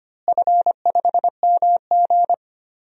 Morse code, heard as a keyed tone, about 25 wpm, 700 Hz. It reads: F5MG
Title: DXCC-100 Call Signs - 4 Characters - Repeated 2x - 25wpm